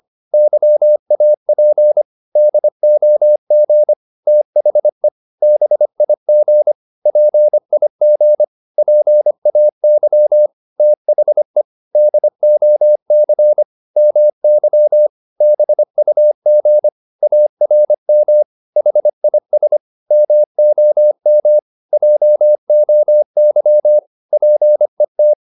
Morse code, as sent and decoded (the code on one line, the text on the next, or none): -.-- .- .--. / -.. --- --. / - .... / -... .. --. / .--. .. --. / .--. .- -.-- / - .... / -.. --- -.-. / -- -.-- / -... ..- --. / .- .-. -- / .... .. ... / -- --- -- / .--- --- -.-- / .--. -
YAP DOG THE BIG PIG PAY THE DOC MY BUG ARM HIS MOM JOY PET